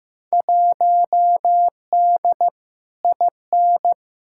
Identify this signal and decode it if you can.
Morse code, heard as a keyed tone, about 15 wpm, 700 Hz. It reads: 1D IN
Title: Field Day Exchanges - 15wpm